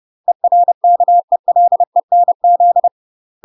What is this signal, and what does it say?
Morse code, heard as a keyed tone, about 30 wpm, 700 Hz. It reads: ERKELENZ
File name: UK MASTER QSOs TRAINING / 30WPM / UK MASTER QSOs 0040